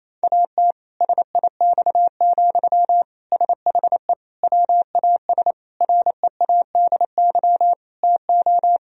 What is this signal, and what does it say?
Morse code, heard as a keyed tone, about 28 wpm, 700 Hz. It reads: AT SIX, SHE WAS READY TO
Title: Sentences from Top 1000 Words - Review Part 2 - 28wpm